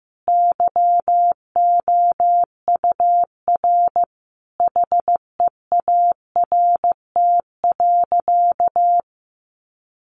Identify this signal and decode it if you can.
Morse code, heard as a keyed tone, about 15 wpm, 700 Hz. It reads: YOUR HEART.
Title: Sentences from Top 1200 Words 15wpm